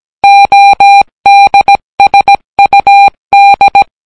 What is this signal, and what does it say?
Morse code, heard as a keyed tone, about 17 wpm, 800 Hz. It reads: ODSUD